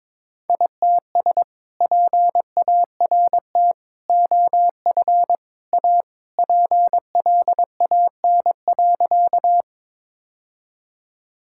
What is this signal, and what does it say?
Morse code, heard as a keyed tone, about 22 wpm, 700 Hz. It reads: ITS PART OF A PLAN.
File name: practice / 22wpm / Sentences with 3-Word Phrases - 22wpm